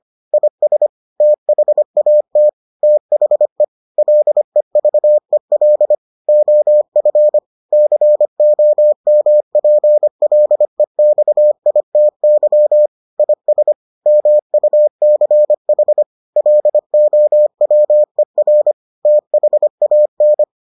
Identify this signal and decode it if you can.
Morse code, heard as a keyed tone, about 25 wpm, 600 Hz. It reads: IS THAT THE LEVEL OF COMPLEXITY IS MUCH LOWER THAN